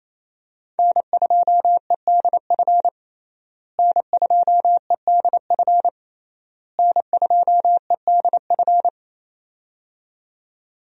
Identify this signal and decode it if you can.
Morse code, heard as a keyed tone, about 27 wpm, 700 Hz. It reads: N2EDF N2EDF N2EDF
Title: USA Call Signs - 5 Characters - Repeated 3x - 28wpm